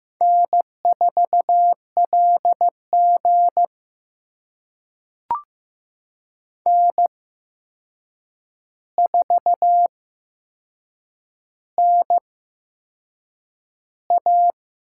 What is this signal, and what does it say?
Morse code, heard as a keyed tone, about 15 wpm, spaced as for 6 wpm, 700 Hz. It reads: N4LG N4NA
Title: USA Call Signs - 4 Characters - 8x Character Spacing - 15wpm